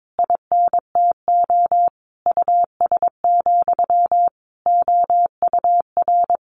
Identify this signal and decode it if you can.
Morse code, heard as a keyed tone, about 22 wpm, 700 Hz. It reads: INTO US, OUR
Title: A Sound Of Thunder 22wpm